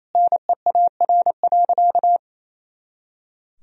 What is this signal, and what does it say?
Morse code, heard as a keyed tone, about 28 wpm, 700 Hz. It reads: NEAR.
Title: Sentences from Top 300 Words and Abbreviations 28wpm